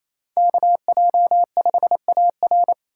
Morse code, heard as a keyed tone, about 28 wpm, 700 Hz. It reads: KJ5AR
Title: USA Call Signs - 2x Character Spacing - 28wpm